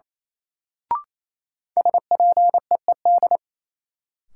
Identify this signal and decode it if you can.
Morse code, heard as a keyed tone, about 28 wpm, 700 Hz. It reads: SPEED
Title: Sentences from Top 800 Words - New Word Drill - 28wpm